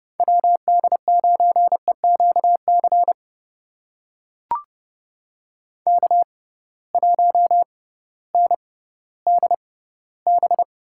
Morse code, heard as a keyed tone, about 31 wpm, spaced as for 16 wpm, 700 Hz. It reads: WD9EQC K1NDB
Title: USA Call Signs - 6x Character Spacing - 30wpm